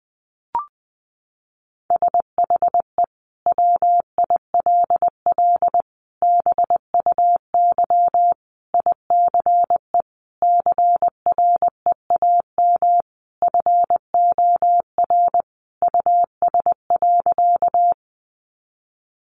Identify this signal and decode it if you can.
Morse code, heard as a keyed tone, about 19 wpm, 700 Hz. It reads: SHE WILL BUY ICE CREAM FOR US.
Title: Sentences from Top 1500 Words 20wpm